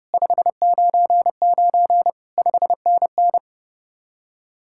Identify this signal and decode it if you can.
Morse code, heard as a keyed tone, about 39 wpm, 700 Hz. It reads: 599 5NN